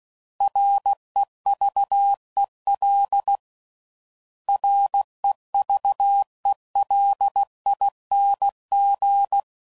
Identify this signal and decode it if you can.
Morse code, heard as a keyed tone, about 16 wpm, 800 Hz. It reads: REVEL REVELING
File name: Prefixes / By First Letter / Rs